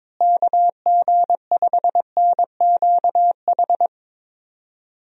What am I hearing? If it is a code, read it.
Morse code, heard as a keyed tone, about 22 wpm, 700 Hz. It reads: KG5NQH
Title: USA Call Signs - 6 Characters - Repeated 2x - 22wpm